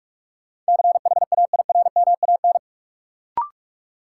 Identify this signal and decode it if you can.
Morse code, heard as a keyed tone, about 46 wpm, 700 Hz. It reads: CHAIRMAN
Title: Sets of 3 Words - New Word Drill - 45wpm